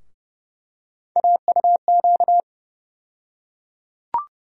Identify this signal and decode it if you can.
Morse code, heard as a keyed tone, about 30 wpm, 700 Hz. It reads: AUQ
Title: USA Call Sign Postfixes - 3 Letters 30wpm